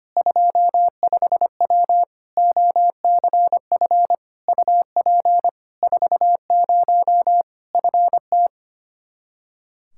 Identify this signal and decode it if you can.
Morse code, heard as a keyed tone, about 25 wpm, 700 Hz. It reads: 25W OCF UP 40 FT
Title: Practice QSO - Exchange 2 - Info - 25wpm